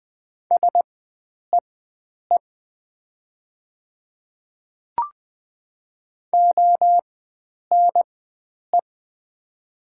Morse code, heard as a keyed tone, about 20 wpm, spaced as for 9 wpm, 700 Hz. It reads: SEE ONE
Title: 4x Character Spacing - Top 100 Words - 20wpm